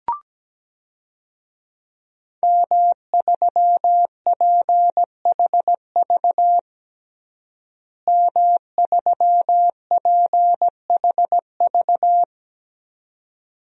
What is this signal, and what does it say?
Morse code, heard as a keyed tone, about 17 wpm, 700 Hz. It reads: M3PHV M3PHV
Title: DXCC-100 Call Signs - 5 Characters - Repeated 2x - 17wpm